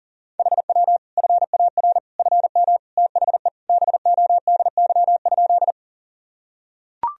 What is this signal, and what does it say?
Morse code, heard as a keyed tone, about 40 wpm, 700 Hz. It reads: HW FAR FM THE BODY?
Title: Sentences from Top 400 Words and Abbreviations 40wpm